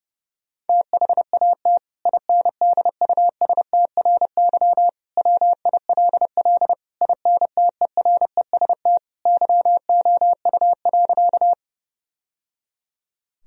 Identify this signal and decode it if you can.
Morse code, heard as a keyed tone, about 30 wpm, 700 Hz. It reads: THAT INDUSTRY WILL INTEREST YOU.
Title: Sentences from Top 1000 Words - Review Part 4 - 30wpm